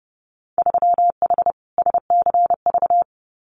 Morse code, heard as a keyed tone, about 30 wpm, 700 Hz. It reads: 3H SCV
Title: Winter Field Day Exchanges Encore 30wpm